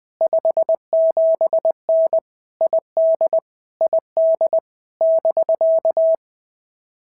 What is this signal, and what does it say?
Morse code, heard as a keyed tone, about 19 wpm, 650 Hz. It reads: 57N ID ID <BK>
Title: POTA QSO Practice - 20wpm